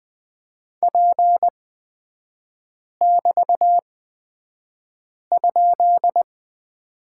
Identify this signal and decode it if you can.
Morse code, heard as a keyed tone, about 20 wpm, spaced as for 5 wpm, 700 Hz. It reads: P<BT>?
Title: 224 Lesson 36e - BT - Three Characters 20wpm